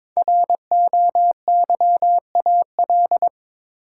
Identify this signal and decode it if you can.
Morse code, heard as a keyed tone, about 22 wpm, 700 Hz. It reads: ROYAL
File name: 5 Letter Words - First 3 Letters Given - 22wpm